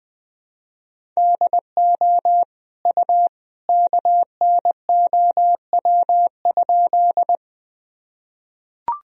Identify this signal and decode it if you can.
Morse code, heard as a keyed tone, about 20 wpm, 700 Hz. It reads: DO U KNOW?
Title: Sentences from Top 100 Words and Abbreviations 20wpm